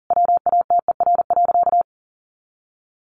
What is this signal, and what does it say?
Morse code, heard as a keyed tone, about 40 wpm, 700 Hz. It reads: WATER.
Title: Sentences from Top 700 Words 40wpm